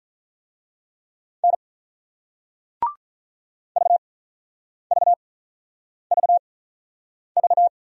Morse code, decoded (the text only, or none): N VVVV